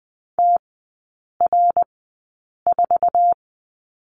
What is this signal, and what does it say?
Morse code, heard as a keyed tone, about 20 wpm, spaced as for 8 wpm, 700 Hz. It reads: TR4
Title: 093 Lesson 16e - Letter c - Three Characters 20wpm